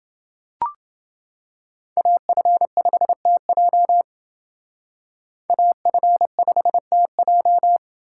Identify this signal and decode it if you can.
Morse code, heard as a keyed tone, about 28 wpm, 700 Hz. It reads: AF5TJ AF5TJ